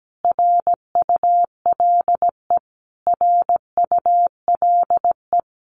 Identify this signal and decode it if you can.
Morse code, heard as a keyed tone, about 17 wpm, 700 Hz. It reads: RULE RULE
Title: Top 500 Words Repeated 17wpm